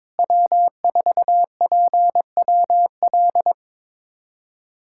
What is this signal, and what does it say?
Morse code, heard as a keyed tone, about 22 wpm, 700 Hz. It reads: W4PWL
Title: USA Callsign DE Callsign - 5 Characters - 22wpm